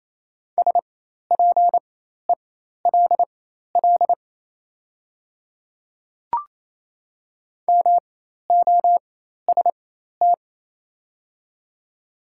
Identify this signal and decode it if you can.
Morse code, heard as a keyed tone, about 28 wpm, spaced as for 13 wpm, 700 Hz. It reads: SPELL MOST
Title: 4x Character Spacing - Top 300 Words - 28wpm